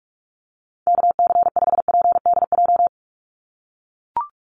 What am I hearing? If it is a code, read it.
Morse code, heard as a keyed tone, about 44 wpm, 700 Hz. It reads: KC5PDJ
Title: USA Call Signs 45wpm